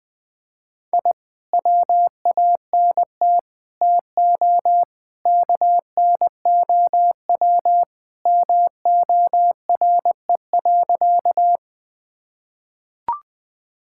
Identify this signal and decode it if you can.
Morse code, heard as a keyed tone, about 20 wpm, 700 Hz. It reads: I WANT TO KNOW MORE.
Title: Sentences with 2-Word Phrases - Spoken Beginning - 20wpm